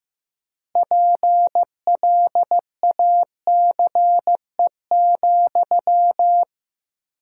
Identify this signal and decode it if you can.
Morse code, heard as a keyed tone, about 15 wpm, 700 Hz. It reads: PLACE,
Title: 4x Word-Spacing - Sentences from Top 400 Words - 15wpm